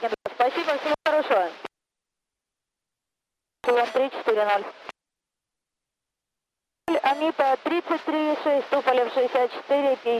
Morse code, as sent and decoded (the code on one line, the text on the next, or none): none